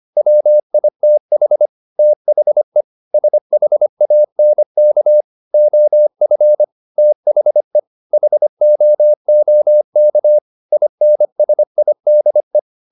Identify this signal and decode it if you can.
Morse code, heard as a keyed tone, about 25 wpm, 600 Hz. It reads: WITH THE SHANK OF THE HOOK INSIDE
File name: the old man and the sea part 1 25wpm